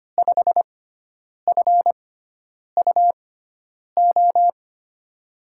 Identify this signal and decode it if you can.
Morse code, heard as a keyed tone, about 25 wpm, spaced as for 9 wpm, 700 Hz. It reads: 5FUO
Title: USA Call Signs - 6 Characters - 6x Character Spacing - 25wpm